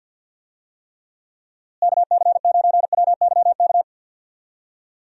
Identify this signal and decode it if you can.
Morse code, heard as a keyed tone, about 50 wpm, 700 Hz. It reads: KC9WYK
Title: USA Call Signs - 6 Characters - Repeated 2x - 50wpm